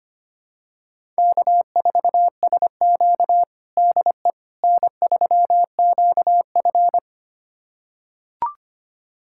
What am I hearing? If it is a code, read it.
Morse code, heard as a keyed tone, about 25 wpm, 700 Hz. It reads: K4SQ DE N3QF
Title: USA Callsign DE Callsign - 4 Characters - 25wpm